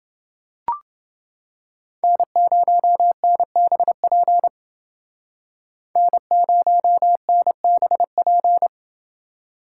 Morse code, heard as a keyed tone, about 28 wpm, 700 Hz. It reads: N0NBP N0NBP